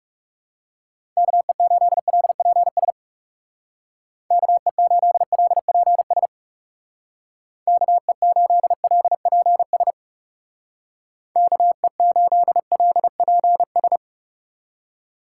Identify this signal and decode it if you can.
Morse code, heard as a keyed tone, about 40 wpm, 700 Hz. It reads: KE8LPS KE8LPS KE8LPS KE8LPS